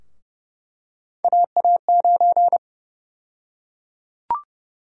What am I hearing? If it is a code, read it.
Morse code, heard as a keyed tone, about 30 wpm, 700 Hz. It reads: AA9
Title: USA Call Sign Prefixes - Repeated 2x - 30wpm